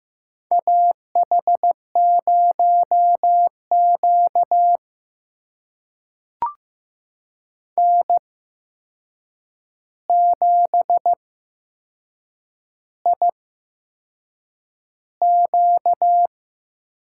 Morse code, heard as a keyed tone, about 15 wpm, spaced as for 6 wpm, 700 Hz. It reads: AH0Q N7IQ
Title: USA Call Signs - 4 Characters - 8x Character Spacing - 15wpm